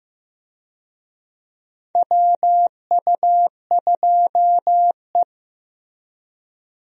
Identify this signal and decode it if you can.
Morse code, heard as a keyed tone, about 15 wpm, 700 Hz. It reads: WU2E
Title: USA Call Signs - 4 Characters - 15wpm